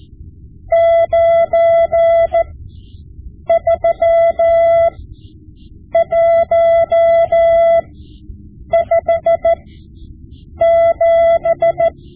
Morse code, heard as a keyed tone, about 14 wpm, spaced as for 7 wpm, 670 Hz. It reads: 93157
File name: Morse Lessons / Morse Lessons / Tape-5B